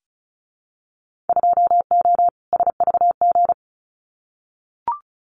Morse code, heard as a keyed tone, about 36 wpm, 700 Hz. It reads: 2O SVG
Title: Winter Field Day Exchanges 35wpm